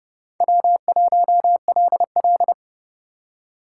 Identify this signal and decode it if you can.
Morse code, heard as a keyed tone, about 32 wpm, 700 Hz. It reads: W1LL
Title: USA Call Signs - 4 Characters - 4x Character Spacing - 30wpm